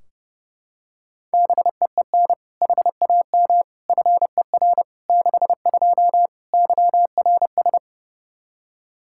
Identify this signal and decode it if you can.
Morse code, heard as a keyed tone, about 30 wpm, 700 Hz. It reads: BEEN HAM FER 62 YRS
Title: K1USN SST QSO Practice 30wpm